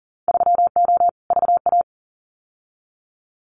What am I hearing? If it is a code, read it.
Morse code, heard as a keyed tone, about 40 wpm, 700 Hz. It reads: 3O VA
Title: Winter Field Day Exchanges 40wpm